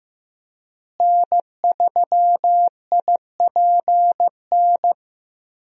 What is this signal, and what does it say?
Morse code, heard as a keyed tone, about 15 wpm, 700 Hz. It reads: N3IPN
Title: USA Call Signs - 5 Characters - Repeated 3x - 15wpm